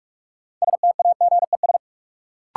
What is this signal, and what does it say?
Morse code, heard as a keyed tone, about 46 wpm, 700 Hz. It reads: STAGES
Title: Sets of 5 Words Encore - New Word Drill - 45wpm